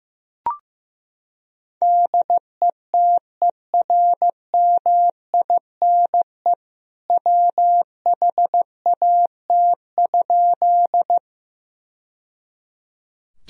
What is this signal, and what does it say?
Morse code, heard as a keyed tone, about 15 wpm, 700 Hz. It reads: DETERMINE WHAT?
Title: Sentences from Top 1000 Words - Review Part 4 - 15wpm